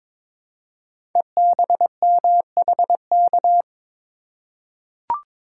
Morse code, heard as a keyed tone, about 22 wpm, 690 Hz. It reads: EBMHK
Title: Code Groups - 5 Letters - 22wpm